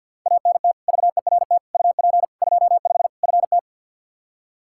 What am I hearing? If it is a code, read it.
Morse code, heard as a keyed tone, about 46 wpm, 700 Hz. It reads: ANT VERT UP 25 FT
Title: Practice QSO - Exchange 2 - Info - 50wpm